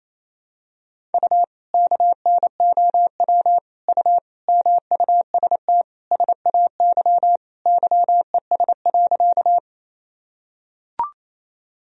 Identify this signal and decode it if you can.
Morse code, heard as a keyed tone, about 28 wpm, 700 Hz. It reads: U KNOW U MUST SAY YES.
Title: Sentences from Top 500 Words and Abbreviations 28wpm